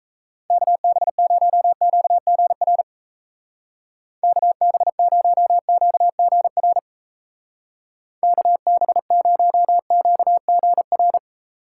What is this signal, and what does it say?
Morse code, heard as a keyed tone, about 34 wpm, 700 Hz. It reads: KB0QGR KB0QGR KB0QGR
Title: Speed Racing - USA Call Signs 28wpm